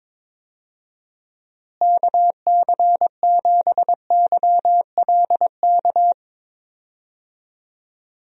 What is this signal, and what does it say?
Morse code, heard as a keyed tone, about 22 wpm, 700 Hz. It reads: KC7YLK
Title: USA Call Signs - 6 Characters - Repeated 2x - 22wpm